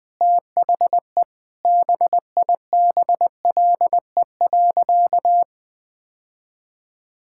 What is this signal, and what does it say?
Morse code, heard as a keyed tone, about 20 wpm, 700 Hz. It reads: THE BIBLE.
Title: Sentences with 9-Word Phrases - 20wpm